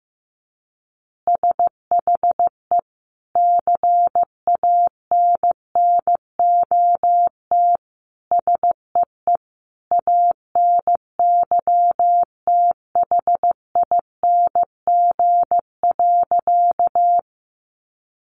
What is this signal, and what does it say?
Morse code, heard as a keyed tone, about 15 wpm, 700 Hz. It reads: SHE CANNOT SEE ANYTHING.
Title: Additional Easy Phrases 15wpm